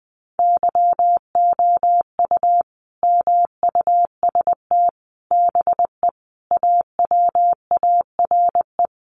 Morse code, heard as a keyed tone, about 20 wpm, 700 Hz. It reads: YOU MUST BE AWARE
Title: Sentences from Top 1200 Words 20wpm